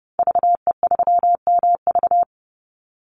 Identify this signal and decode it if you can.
Morse code, heard as a keyed tone, about 30 wpm, 700 Hz. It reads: VE3MV
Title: Call Signs - 30wpm